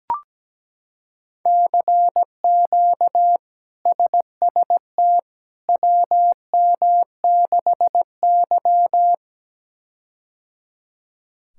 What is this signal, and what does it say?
Morse code, heard as a keyed tone, about 17 wpm, 700 Hz. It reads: CQ SST WM6Y
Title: K1USN SST QSO Practice - 17wpm